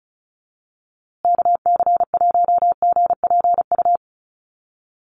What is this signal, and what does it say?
Morse code, heard as a keyed tone, about 34 wpm, 700 Hz. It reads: KC1GPU
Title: USA Call Signs Repeated 2x 35wpm